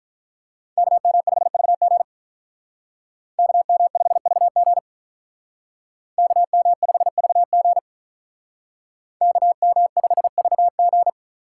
Morse code, heard as a keyed tone, about 46 wpm, 700 Hz. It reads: KM5VG KM5VG KM5VG KM5VG